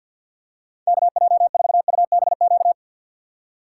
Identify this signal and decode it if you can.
Morse code, heard as a keyed tone, about 52 wpm, 700 Hz. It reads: KJ4UBQ
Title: USA Call Signs - 2x Character Spacing - 50wpm